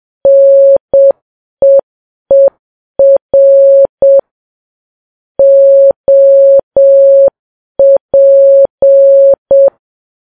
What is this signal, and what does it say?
Morse code, heard as a keyed tone, about 7 wpm, 550 Hz. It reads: NEER OP
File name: Files Web tls / Morse on9fv / morse7kps